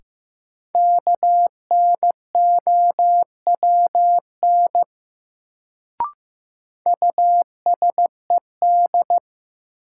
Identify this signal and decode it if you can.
Morse code, heard as a keyed tone, about 15 wpm, 700 Hz. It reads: KNOWN USED
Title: Sentences from Top 600 Words - New Word Drill - 15wpm